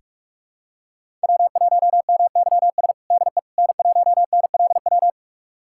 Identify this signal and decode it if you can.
Morse code, heard as a keyed tone, about 43 wpm, 700 Hz. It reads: W1MYS DE N1NLW
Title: USA Callsign DE Callsign - 5 Characters - 45wpm